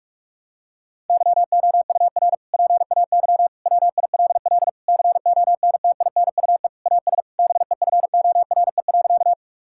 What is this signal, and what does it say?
Morse code, heard as a keyed tone, about 46 wpm, 700 Hz. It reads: YOUR PAY WILL CONTINUE AS BEFORE.
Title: Sentences from Top 1000 Words - Review Part 6 - 45wpm